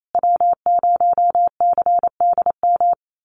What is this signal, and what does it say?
Morse code, heard as a keyed tone, about 28 wpm, 700 Hz. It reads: W0CDM
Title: USA Call Signs Repeated 3x 28wpm